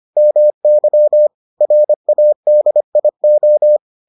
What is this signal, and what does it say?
Morse code, heard as a keyed tone, about 25 wpm, 600 Hz. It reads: MY RADIO